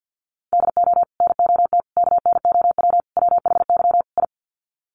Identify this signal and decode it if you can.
Morse code, heard as a keyed tone, about 50 wpm, 700 Hz. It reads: DO NOT KNOW WHY I
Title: Sets of 5 Words Encore 50wpm